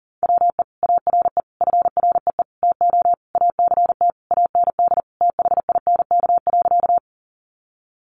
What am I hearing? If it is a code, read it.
Morse code, heard as a keyed tone, about 40 wpm, 700 Hz. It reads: WE ARE FREE TO ACT AND THINK.
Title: Sentences from Top 500 Words 40wpm